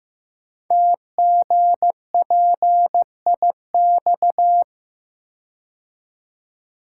Morse code, heard as a keyed tone, about 15 wpm, 700 Hz. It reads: TGPIX
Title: Code Groups - 5 Letters - 15wpm